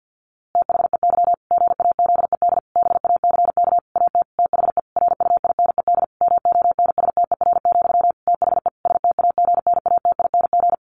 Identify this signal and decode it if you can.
Morse code, heard as a keyed tone, about 50 wpm, 700 Hz. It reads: THEY GAZED BACK AT THE RUINED MONSTER, THE STAGNATING
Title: A Sound Of Thunder 50wpm